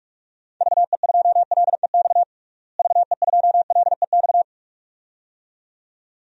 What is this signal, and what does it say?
Morse code, heard as a keyed tone, about 46 wpm, 700 Hz. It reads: VE2LEX VE2LEX
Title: DXCC-100 Call Signs - 6 Characters - Repeated 2x - 45wpm